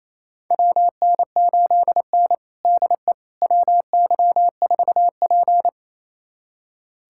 Morse code, heard as a keyed tone, about 28 wpm, 700 Hz. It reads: WN8N DE WY4P
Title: USA Callsign DE Callsign - 4 Characters - 28wpm